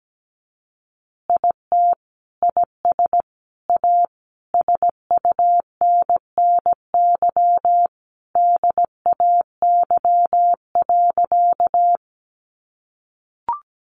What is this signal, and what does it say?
Morse code, heard as a keyed tone, about 17 wpm, 700 Hz. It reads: IT IS A SUNNY DAY.
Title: Easy Phrases 17wpm